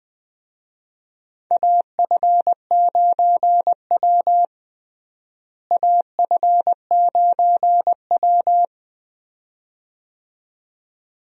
Morse code, heard as a keyed tone, about 20 wpm, 700 Hz. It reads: AF9W AF9W
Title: USA Call Signs - 4 Characters - Repeated 2x - 20wpm